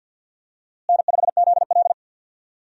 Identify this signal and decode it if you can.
Morse code, heard as a keyed tone, about 51 wpm, 700 Hz. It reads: N5ZL
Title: USA Call Signs - 4 Characters - Repeated 2x - 50wpm